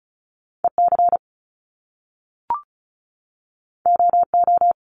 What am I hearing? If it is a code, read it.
Morse code, heard as a keyed tone, about 36 wpm, 700 Hz. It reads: EC OO
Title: Top Two Letter Combinations in Words 35wpm